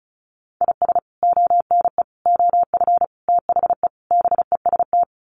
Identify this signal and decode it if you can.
Morse code, heard as a keyed tone, about 34 wpm, 700 Hz. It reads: IS ONE OF THE BEST
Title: Sets of 5 Words 35wpm